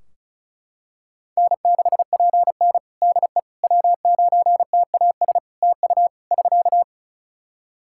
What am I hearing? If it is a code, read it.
Morse code, heard as a keyed tone, about 34 wpm, 700 Hz. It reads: N6PN DE W9TAS TU <SK>
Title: Practice QSO - Exchange 4 - Ending - 35wpm